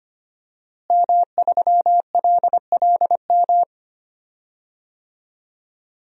Morse code, heard as a keyed tone, about 25 wpm, 700 Hz. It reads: M3LLM
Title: DXCC-100 Call Signs - 5 Characters - Repeated 2x - 25wpm